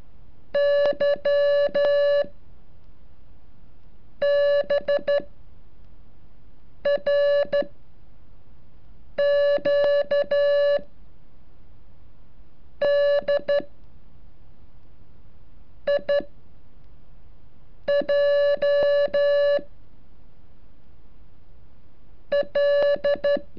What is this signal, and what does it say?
Morse code, heard as a keyed tone, about 13 wpm, spaced as for 4 wpm, 570 Hz. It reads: YBRQDIJ L
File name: curso / ejercicio 37